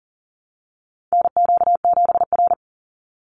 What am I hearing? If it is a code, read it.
Morse code, heard as a keyed tone, about 40 wpm, 700 Hz. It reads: NQ7R
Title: NA SOTA Chaser Callsigns 40wpm